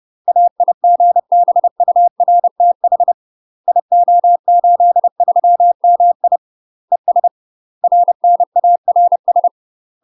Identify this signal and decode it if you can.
Morse code, heard as a keyed tone, about 30 wpm, 700 Hz. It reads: AIGBURTH IO83MI ES RNARS
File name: UK MASTER QSOs TRAINING / 30WPM / UK MASTER QSOs 0021